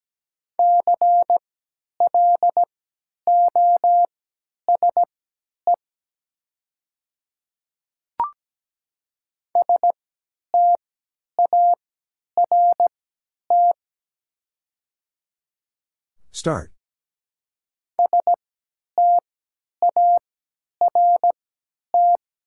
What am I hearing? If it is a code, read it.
Morse code, heard as a keyed tone, about 17 wpm, spaced as for 10 wpm, 700 Hz. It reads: CLOSE START START
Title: 3x Character Spacing - Top 300 Words - 17wpm